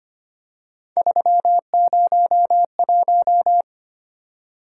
Morse code, heard as a keyed tone, about 25 wpm, 700 Hz. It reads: 301
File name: Three Letter-Number Combinations - 25wpm